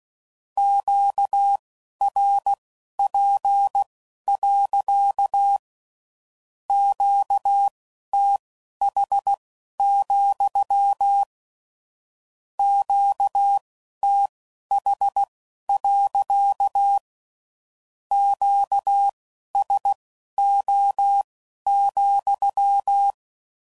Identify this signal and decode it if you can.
Morse code, heard as a keyed tone, about 16 wpm, spaced as for 11 wpm, 800 Hz. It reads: QRP. QTH, QTH. QSO,
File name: CW Code Groups @15WPM / Q Calls / Q Calls sppc